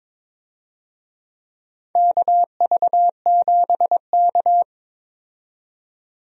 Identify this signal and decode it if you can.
Morse code, heard as a keyed tone, about 22 wpm, 700 Hz. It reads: KV7K
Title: USA Call Signs - 4 Characters - 22wpm